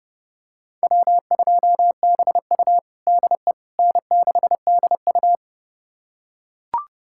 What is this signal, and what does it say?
Morse code, heard as a keyed tone, about 32 wpm, 700 Hz. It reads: W2BU DE N6DU
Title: USA Callsign DE Callsign - 4 Characters - 30wpm